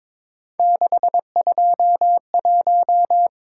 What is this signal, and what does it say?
Morse code, heard as a keyed tone, about 26 wpm, 700 Hz. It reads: 621
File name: Speed-Racing - Three Letter-Number Combinations - 22wpm